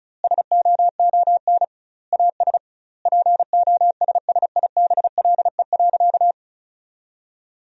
Morse code, heard as a keyed tone, about 36 wpm, 700 Hz. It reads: SOON AS POSSIBLE.
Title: 2x Word-Spacing - Sentences from Top 500 Words - 35wpm